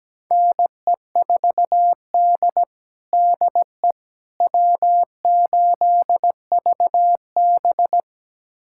Morse code, heard as a keyed tone, about 17 wpm, 700 Hz. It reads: NE4D DE W8VB
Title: USA Callsign DE Callsign - 4 Characters - 17wpm